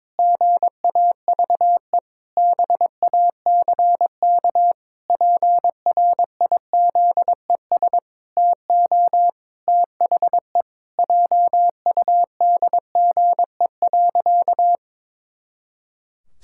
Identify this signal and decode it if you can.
Morse code, heard as a keyed tone, about 22 wpm, 700 Hz. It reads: GAVE BACK PRIZES TO THE JUDGE.
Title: Pangram Challenge v2 - 22wpm